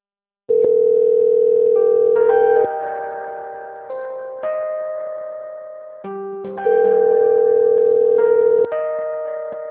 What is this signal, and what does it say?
Ringback tone (North America), 440 + 480 Hz.
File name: out- - - - - .84468